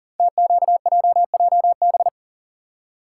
Morse code, heard as a keyed tone, about 40 wpm, 700 Hz. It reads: TQJJB
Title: Code Groups - 5 Letter:Numbers - 40wpm